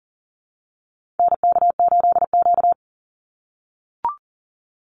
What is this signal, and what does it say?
Morse code, heard as a keyed tone, about 40 wpm, 700 Hz. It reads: NK8Q
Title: NA SOTA Activator Callsigns 40wpm